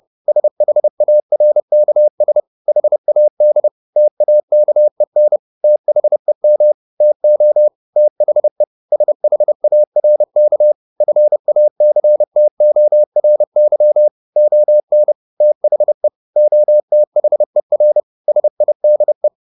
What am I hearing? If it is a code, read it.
Morse code, heard as a keyed tone, about 30 wpm, 600 Hz. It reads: SHARKS HAD TAKEN THEM TO THE SHARK FACTORY ON THE OTHER SIDE
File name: the old man and the sea / the old man and the sea 1 30wpm